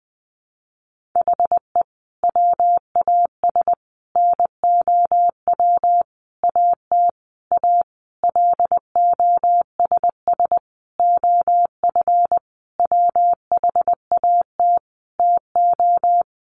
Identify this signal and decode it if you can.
Morse code, heard as a keyed tone, about 20 wpm, 700 Hz. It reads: HE WAS NOW AT A LOSS OF WHAT TO
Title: Sentences from Top 1400 Words 20wpm